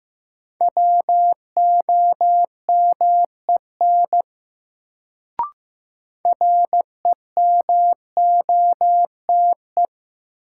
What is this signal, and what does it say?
Morse code, heard as a keyed tone, about 15 wpm, 700 Hz. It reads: WOMEN REMOTE
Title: Sentences from Top 1300 Words - New Word Drill - 15wpm